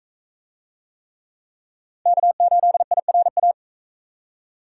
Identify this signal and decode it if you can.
Morse code, heard as a keyed tone, about 42 wpm, 700 Hz. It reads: K8IRA